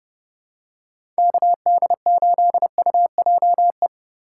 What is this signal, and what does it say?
Morse code, heard as a keyed tone, about 31 wpm, 700 Hz. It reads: KD8UJE
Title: USA Call Signs - 6 Characters - 4x Character Spacing - 30wpm